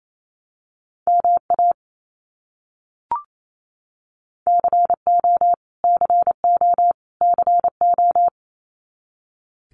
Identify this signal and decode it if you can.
Morse code, heard as a keyed tone, about 28 wpm, 700 Hz. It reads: MA CO CO CO
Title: US States Repeated 3x 28wpm